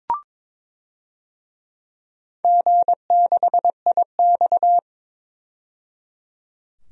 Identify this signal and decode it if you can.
Morse code, heard as a keyed tone, about 22 wpm, 700 Hz. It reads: G6IX
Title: DXCC-100 Call Signs - 4 Characters - 22wpm